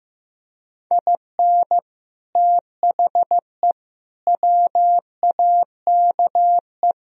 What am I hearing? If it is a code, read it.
Morse code, heard as a keyed tone, about 16 wpm, 700 Hz. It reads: IN THE WAKE
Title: Speed-Racing - Sets of 5 Words 15wpm